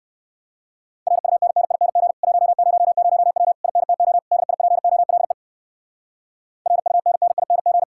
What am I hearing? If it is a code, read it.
Morse code, heard as a keyed tone, about 56 wpm, 700 Hz. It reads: RUNNING 100W INTO DIPOLE RUNNING
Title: Speed-Racing Common QSO Phrases - 45wpm